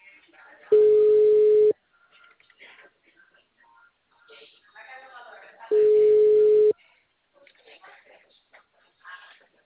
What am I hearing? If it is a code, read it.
Ringback tone (Europe, 425 Hz), with speech.